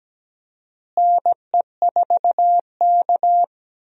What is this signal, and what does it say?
Morse code, heard as a keyed tone, about 17 wpm, 700 Hz. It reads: NE4K
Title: USA Call Signs - 4 Characters - Repeated 3x - 17wpm